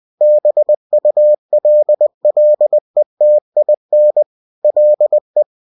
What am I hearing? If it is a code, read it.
Morse code, heard as a keyed tone, about 20 wpm, 600 Hz. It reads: BULLETIN LE